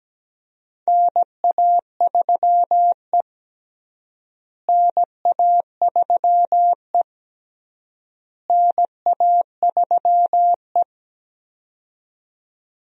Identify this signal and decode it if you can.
Morse code, heard as a keyed tone, about 17 wpm, 700 Hz. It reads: NA3E NA3E NA3E
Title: USA Call Signs - 4 Characters - Repeated 3x - 17wpm